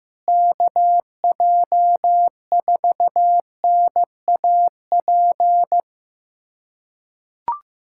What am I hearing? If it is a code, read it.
Morse code, heard as a keyed tone, about 15 wpm, 700 Hz. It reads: KJ4NAP
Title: USA Callsign DE Callsign - 6 Characters - 15wpm